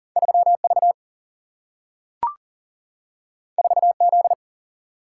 Morse code, heard as a keyed tone, about 40 wpm, 700 Hz. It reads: 3 V 4 Z
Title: Single-Letter Single-Number 40wpm